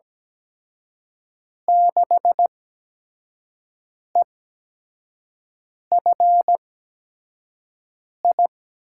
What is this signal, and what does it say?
Morse code, heard as a keyed tone, about 17 wpm, spaced as for 5 wpm, 700 Hz. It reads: 6EFI
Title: USA Call Signs - 5 Characters - 8x Character Spacing - 17wpm